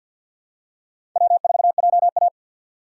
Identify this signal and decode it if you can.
Morse code, heard as a keyed tone, about 53 wpm, 700 Hz. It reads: W4JA
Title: USA Call Signs - 4 Characters - 6x Character Spacing - 50wpm